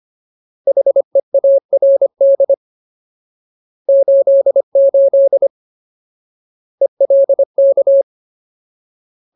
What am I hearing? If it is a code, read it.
Morse code, heard as a keyed tone, about 25 wpm, 550 Hz. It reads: HEARD 88 ELK
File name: Ultimate Word List / 25WPM Ultimate